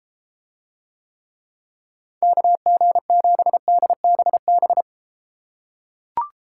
Morse code, heard as a keyed tone, about 31 wpm, 700 Hz. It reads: KG7DBB